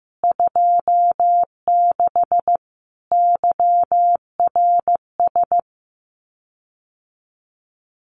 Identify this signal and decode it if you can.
Morse code, heard as a keyed tone, about 15 wpm, 700 Hz. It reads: 26 YRS
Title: Common QSO Phrases Encore 15wpm